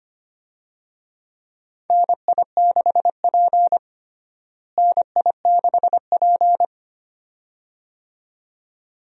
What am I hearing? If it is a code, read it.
Morse code, heard as a keyed tone, about 25 wpm, 700 Hz. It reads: NI6P NI6P
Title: USA Call Signs - 4 Characters - Repeated 2x - 25wpm